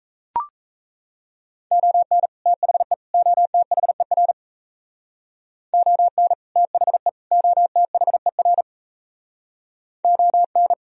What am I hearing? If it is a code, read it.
Morse code, heard as a keyed tone, about 37 wpm, 700 Hz. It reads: ON THE OTHER ON THE OTHER ON